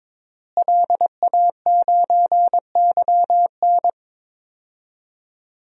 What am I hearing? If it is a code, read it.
Morse code, heard as a keyed tone, about 22 wpm, 700 Hz. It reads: LA9YN
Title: DXCC-100 Call Signs - 5 Characters - Repeated 2x - 22wpm